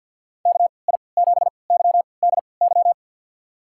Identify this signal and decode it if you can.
Morse code, heard as a keyed tone, about 52 wpm, spaced as for 29 wpm, 700 Hz. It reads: KI7YDY
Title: USA Call Signs - 3x Character Spacing - 50wpm